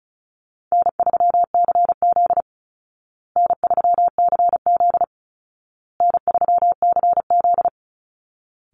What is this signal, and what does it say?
Morse code, heard as a keyed tone, about 34 wpm, 700 Hz. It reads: N3CZ N3CZ N3CZ
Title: Call Signs Repeated 2x - 35wpm